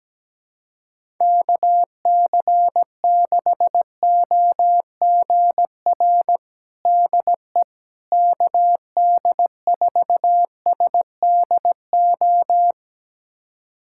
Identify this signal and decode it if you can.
Morse code, heard as a keyed tone, about 17 wpm, 700 Hz. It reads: KC6OGR DE KD4SDO
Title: USA Callsign DE Callsign 17wpm